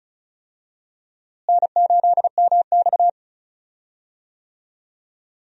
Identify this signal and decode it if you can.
Morse code, heard as a keyed tone, about 36 wpm, 700 Hz. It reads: N8MX
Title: USA Call Signs - 4 Characters - Repeated 2x - 35wpm